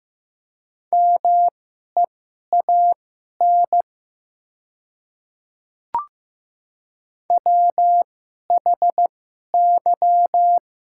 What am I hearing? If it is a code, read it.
Morse code, heard as a keyed tone, about 15 wpm, spaced as for 11 wpm, 700 Hz. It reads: MEAN WHY
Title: 2x Character Spacing - Top 300 Words - 15wpm